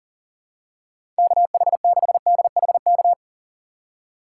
Morse code, heard as a keyed tone, about 39 wpm, 700 Hz. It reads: KH6DHK
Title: K1USN SST QSO Practice - 40wpm